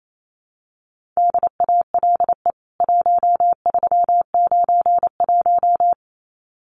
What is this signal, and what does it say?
Morse code, heard as a keyed tone, about 28 wpm, 700 Hz. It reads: DALE 1391
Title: CWT Exchange 28wpm